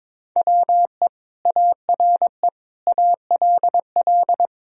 Morse code, heard as a keyed tone, about 22 wpm, 700 Hz. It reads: WE ARE ALL
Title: Sentences from Top 1000 Words - Review Part 2 - 22wpm